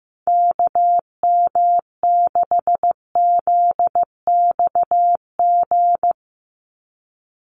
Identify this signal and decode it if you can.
Morse code, heard as a keyed tone, about 15 wpm, 700 Hz. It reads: KM6ZXG
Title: USA Call Signs 15wpm